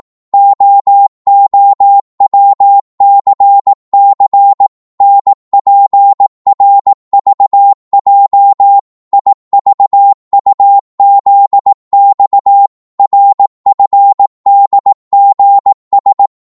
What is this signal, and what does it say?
Morse code, heard as a keyed tone, about 18 wpm, 800 Hz. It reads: OOWCC NPRVJ IVUZX RFDGS
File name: Morse / ltr-18wpm